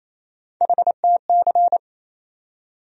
Morse code, heard as a keyed tone, about 28 wpm, 700 Hz. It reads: HTC
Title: USA Call Sign Postfixes - 3 Letters 28wpm